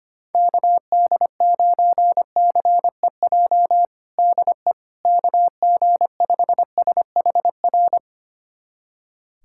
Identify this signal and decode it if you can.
Morse code, heard as a keyed tone, about 25 wpm, 700 Hz. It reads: KD9CEJ DE KG5SHR
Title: USA Callsign DE Callsign - 6 Characters - 25wpm